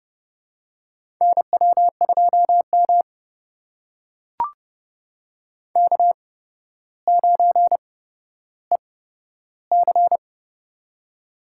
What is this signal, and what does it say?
Morse code, heard as a keyed tone, about 30 wpm, spaced as for 13 wpm, 700 Hz. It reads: NW2M K9EC
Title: USA Call Signs - 4 Characters - 8x Character Spacing - 30wpm